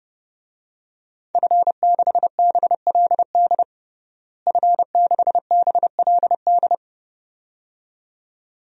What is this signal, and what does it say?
Morse code, heard as a keyed tone, about 30 wpm, 700 Hz. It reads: F6BLD F6BLD
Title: DXCC-100 Call Signs - 5 Characters - Repeated 2x - 30wpm